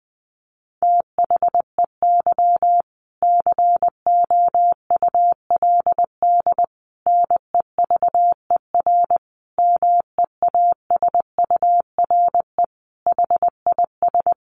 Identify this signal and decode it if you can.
Morse code, heard as a keyed tone, about 20 wpm, 700 Hz. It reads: THEY COULD NEVER MEASURE HIS
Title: Sentences from Top 1200 Words 20wpm